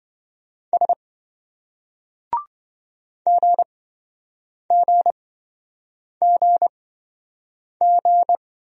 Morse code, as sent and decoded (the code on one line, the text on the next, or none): ... / --. --. --. --.
S GGGG